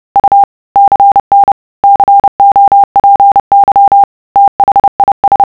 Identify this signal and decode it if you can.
Morse code, heard as a keyed tone, about 30 wpm, 800 Hz. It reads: U CN COPY THIS